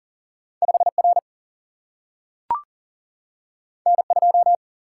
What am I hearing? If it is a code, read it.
Morse code, heard as a keyed tone, about 42 wpm, 700 Hz. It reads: 5R N2
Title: Single-Letter Single-Number 40wpm